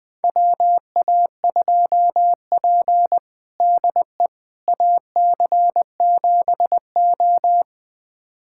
Morse code, heard as a keyed tone, about 21 wpm, 700 Hz. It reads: WA2P DE AC7O
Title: USA Callsign DE Callsign - 4 Characters - 20wpm